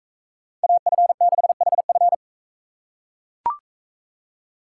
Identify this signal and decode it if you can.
Morse code, heard as a keyed tone, about 42 wpm, 700 Hz. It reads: AF6HF